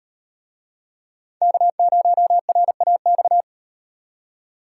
Morse code, heard as a keyed tone, about 37 wpm, 700 Hz. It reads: K0RAX